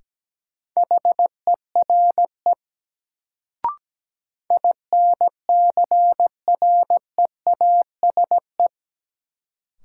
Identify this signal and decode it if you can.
Morse code, heard as a keyed tone, about 17 wpm, 700 Hz. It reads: HERE INCREASE
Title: Sets of 5 Words - New Word Drill - 17wpm